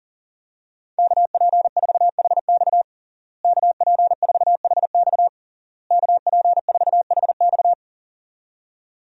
Morse code, heard as a keyed tone, about 40 wpm, 700 Hz. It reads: KP4HX KP4HX KP4HX
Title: USA Call Signs - 5 Characters - Repeated 3x - 40wpm